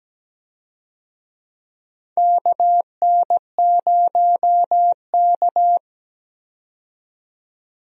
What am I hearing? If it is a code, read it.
Morse code, heard as a keyed tone, about 17 wpm, 700 Hz. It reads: KN0K
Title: USA Call Signs - 4 Characters - 17wpm